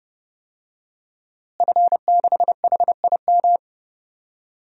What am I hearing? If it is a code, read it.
Morse code, heard as a keyed tone, about 30 wpm, 700 Hz. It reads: F6HIM
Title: DXCC-100 Call Signs - 5 Characters - Repeated 2x - 30wpm